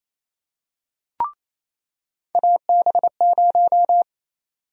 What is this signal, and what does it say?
Morse code, heard as a keyed tone, about 27 wpm, 700 Hz. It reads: AB0
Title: USA Call Sign Prefixes - 28wpm